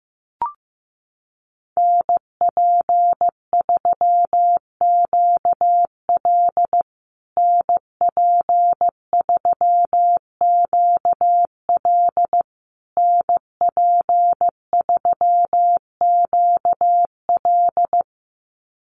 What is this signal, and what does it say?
Morse code, heard as a keyed tone, about 15 wpm, 700 Hz. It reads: NP3QL NP3QL NP3QL
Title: USA Call Signs Repeated 3x 15wpm